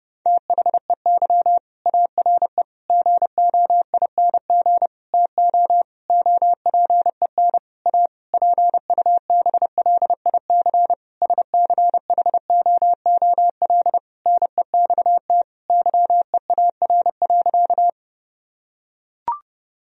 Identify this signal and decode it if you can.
Morse code, heard as a keyed tone, about 32 wpm, 700 Hz. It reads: THEY ARE GOING TO OPEN A PUBLIC SCHOOL NEXT YEAR.
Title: Sentences from Top 1000 Words - Review Part 3 - 30wpm